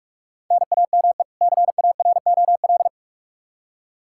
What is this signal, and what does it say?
Morse code, heard as a keyed tone, about 47 wpm, 700 Hz. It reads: NAME CAROL
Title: Practice QSO - Exchange 1 - Beginning - 45wpm